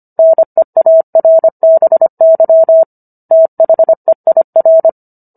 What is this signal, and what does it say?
Morse code, heard as a keyed tone, about 25 wpm, 650 Hz. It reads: NEARBY THEIR
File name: Book One / 16-Chapter 16 - 25 WPM